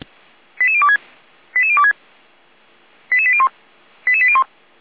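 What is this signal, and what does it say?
Five-tone selective call (ZVEI), heard as quick runs of five tones, tones of about 70 ms. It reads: 88026 88026 88801 88801
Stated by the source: ZVEI five-tone call 89033 received by SDR, from the feuerwehr-orlamuende (fire brigade Orlamünde) server.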